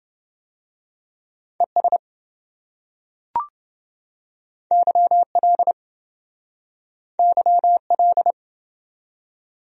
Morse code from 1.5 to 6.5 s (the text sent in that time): ES YL